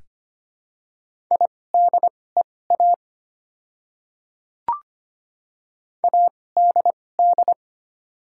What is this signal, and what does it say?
Morse code, heard as a keyed tone, about 25 wpm, spaced as for 18 wpm, 700 Hz. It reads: IDEA ADD
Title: 2x Character Spacing - Top 300 Words - 25wpm